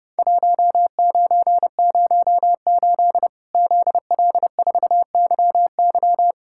Morse code, heard as a keyed tone, about 30 wpm, 700 Hz. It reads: 1908 ZL4YY
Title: Worked All Europe DX Contest - QTC Exchange - 30wpm